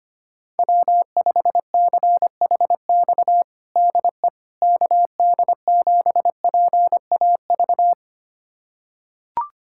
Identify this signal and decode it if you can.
Morse code, heard as a keyed tone, about 25 wpm, 700 Hz. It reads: W5CHX DE KD7PAV
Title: USA Callsign DE Callsign 25wpm